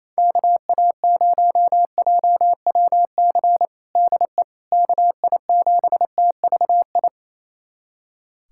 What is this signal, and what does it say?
Morse code, heard as a keyed tone, about 28 wpm, 700 Hz. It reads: KA0JWC DE KI7TVI
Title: USA Callsign DE Callsign - 6 Characters - 28wpm